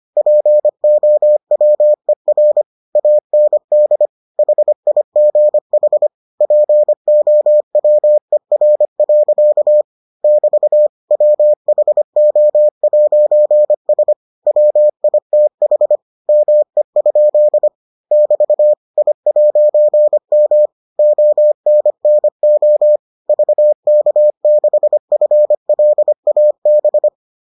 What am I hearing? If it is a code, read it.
Morse code, heard as a keyed tone, about 25 wpm, 600 Hz. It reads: POWER AND HIGH POWER. <BT> WHO'S WITH ME? <BT> I'M ONNO VK6FLAB